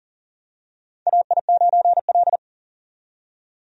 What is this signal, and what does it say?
Morse code, heard as a keyed tone, about 40 wpm, 700 Hz. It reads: AI9L
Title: USA Call Signs - 4 Characters - 40wpm